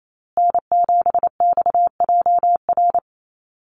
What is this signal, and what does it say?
Morse code, heard as a keyed tone, about 28 wpm, 690 Hz. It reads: N7XJR
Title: USA Call Signs Repeated 3x 28wpm